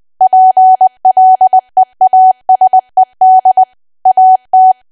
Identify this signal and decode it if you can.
Morse code, heard as a keyed tone, about 20 wpm, 750 Hz. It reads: PLEASED AT
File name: Archive / 20 WPM / 20WPM